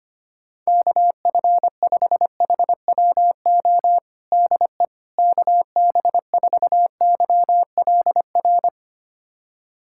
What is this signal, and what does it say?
Morse code, heard as a keyed tone, about 25 wpm, 700 Hz. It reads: KF5HWO DE KB4YLR
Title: USA Callsign DE Callsign - 6 Characters - 25wpm